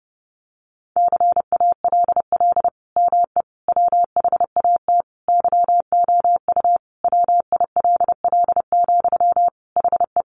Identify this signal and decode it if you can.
Morse code, heard as a keyed tone, about 28 wpm, 700 Hz. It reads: CALL ME WHAT YOU WILL, HE
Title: Sentences from Top 200 Words 30wpm